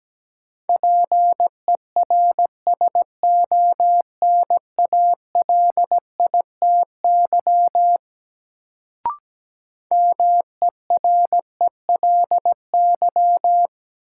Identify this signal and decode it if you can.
Morse code, heard as a keyed tone, about 17 wpm, 700 Hz. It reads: PERSONALITY MERELY
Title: Sentences from Top 1400 Words - New Word Drill Alt - 17wpm